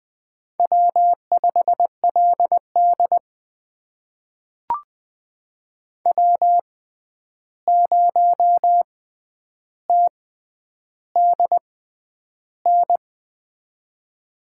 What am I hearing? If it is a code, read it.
Morse code, heard as a keyed tone, about 20 wpm, spaced as for 10 wpm, 700 Hz. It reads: W5LD W0TDN
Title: USA Call Signs - 6x Character Spacing - 20wpm